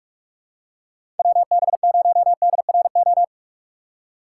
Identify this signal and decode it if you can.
Morse code, heard as a keyed tone, about 45 wpm, 700 Hz. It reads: WB0DRO
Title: USA Call Signs - 6 Characters - 6x Character Spacing - 45wpm